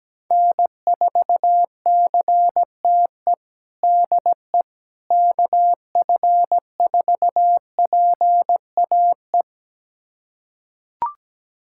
Morse code, heard as a keyed tone, about 17 wpm, 700 Hz. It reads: N4CTE DE KF4PAE
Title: USA Callsign DE Callsign 17wpm